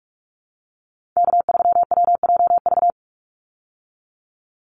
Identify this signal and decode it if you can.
Morse code, heard as a keyed tone, about 44 wpm, 700 Hz. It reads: K3WJV
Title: Call Signs Repeated 3x - 45wpm